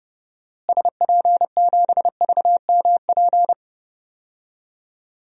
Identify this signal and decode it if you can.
Morse code, heard as a keyed tone, about 31 wpm, 700 Hz. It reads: SP7VMP
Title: DXCC-100 Call Signs - 6 Characters - Repeated 3x - 30wpm